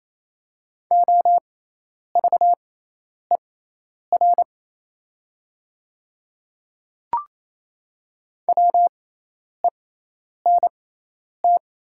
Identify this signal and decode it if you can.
Morse code, heard as a keyed tone, about 28 wpm, spaced as for 10 wpm, 700 Hz. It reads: OVER WENT
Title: 6x Character Spacing - Top 200 Words - 28wpm